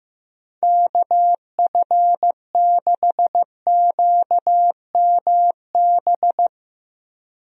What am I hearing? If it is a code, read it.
Morse code, heard as a keyed tone, about 15 wpm, 700 Hz. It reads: KF6QMB
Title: USA Call Signs - 6 Characters - Repeated 3x - 15wpm